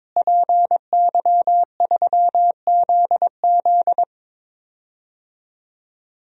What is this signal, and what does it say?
Morse code, heard as a keyed tone, about 22 wpm, 700 Hz. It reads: PY3ZZ
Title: DXCC-100 Call Signs - 5 Characters - Repeated 2x - 22wpm